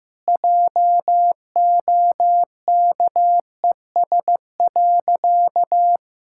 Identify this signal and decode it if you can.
Morse code, heard as a keyed tone, about 15 wpm, 700 Hz. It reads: JOKES.
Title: Pangram Challenge v2 - 15wpm